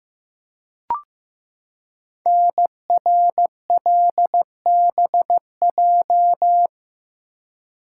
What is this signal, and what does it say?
Morse code, heard as a keyed tone, about 15 wpm, 700 Hz. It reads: NRLBJ
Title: Code Groups - 5 Letters - 15wpm